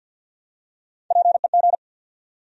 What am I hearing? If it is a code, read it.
Morse code, heard as a keyed tone, about 54 wpm, 700 Hz. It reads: PEG
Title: USA Common Names - Max 3 Letters - 50wpm